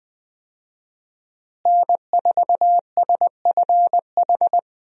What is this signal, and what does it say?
Morse code, heard as a keyed tone, about 20 wpm, 700 Hz. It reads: N4SFH
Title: USA Call Signs - 5 Characters - Repeated 2x - 20wpm